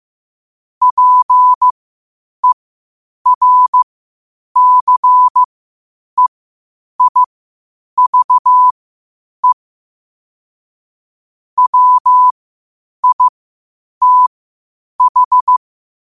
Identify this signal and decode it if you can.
Morse code, heard as a keyed tone, about 15 wpm, spaced as for 8 wpm, 1000 Hz. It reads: PERCEIVE WITH
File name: qotdfiles / 8wpm